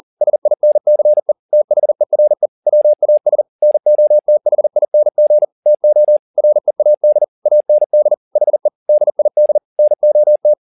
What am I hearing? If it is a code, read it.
Morse code, heard as a keyed tone, about 40 wpm, 600 Hz. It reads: SINCE THERE WAS NOTHING TO READ AND HE DID NOT
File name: the old man and the sea 2 40wpm